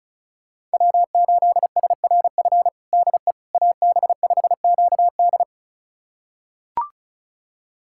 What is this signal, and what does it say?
Morse code, heard as a keyed tone, about 36 wpm, 690 Hz. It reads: W8SRF DE AB5QD
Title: USA Callsign DE Callsign - 5 Characters - 35wpm